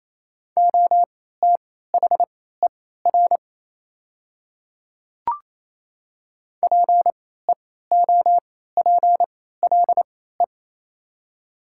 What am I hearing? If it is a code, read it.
Morse code, heard as a keyed tone, about 28 wpm, spaced as for 16 wpm, 700 Hz. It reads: OTHER PEOPLE
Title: 3x Character Spacing - Top 100 Words - 28wpm